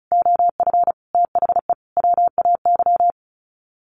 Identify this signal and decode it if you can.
Morse code, heard as a keyed tone, about 34 wpm, 700 Hz. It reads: OF THE WAY
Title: Sets of 5 Words Encore2 35wpm